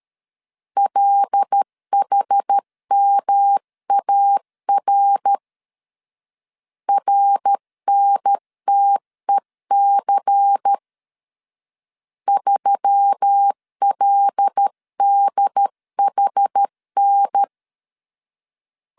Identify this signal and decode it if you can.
Morse code, heard as a keyed tone, about 13 wpm, 800 Hz. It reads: LHMAR RNTEC 3LDHN
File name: CW / DET / 1 6 1500C S C 12wpm 3533